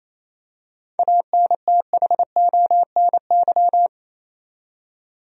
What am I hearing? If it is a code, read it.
Morse code, heard as a keyed tone, about 28 wpm, 700 Hz. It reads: ANTHONY
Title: USA Common Names - 28wpm